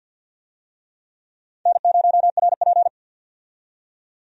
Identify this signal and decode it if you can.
Morse code, heard as a keyed tone, about 51 wpm, 700 Hz. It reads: N0RP
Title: USA Call Signs - 4 Characters - 50wpm